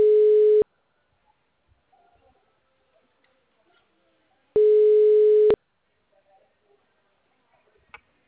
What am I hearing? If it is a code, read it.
Ringback tone (Europe, 425 Hz).
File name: out- -2004- - - .74904